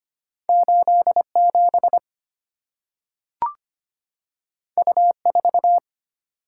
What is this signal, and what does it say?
Morse code, heard as a keyed tone, about 25 wpm, 700 Hz. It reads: 87 U4
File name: Two Letter-Number Combinations - 25wpm